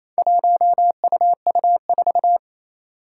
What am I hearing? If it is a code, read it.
Morse code, heard as a keyed tone, about 27 wpm, 700 Hz. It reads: 1UU4
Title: Four Challenging Letter-Number Combinations 28wpm